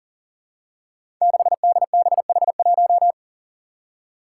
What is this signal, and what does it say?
Morse code, heard as a keyed tone, about 44 wpm, 700 Hz. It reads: 6DBH1
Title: Five Challenging Letter-Number Combinations 40wpm